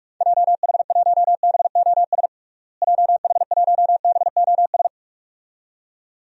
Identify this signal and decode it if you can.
Morse code, heard as a keyed tone, about 42 wpm, 690 Hz. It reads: JH1BOS JH1BOS
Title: DXCC-100 Call Signs - 6 Characters - Repeated 2x - 45wpm